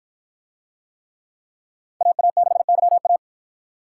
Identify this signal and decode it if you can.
Morse code, heard as a keyed tone, about 53 wpm, 700 Hz. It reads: AA6YA